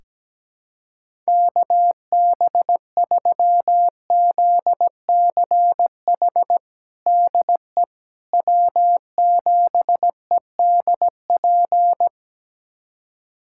Morse code, heard as a keyed tone, about 17 wpm, 700 Hz. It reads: KB3ZCH DE W7EDP
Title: USA Callsign DE Callsign 17wpm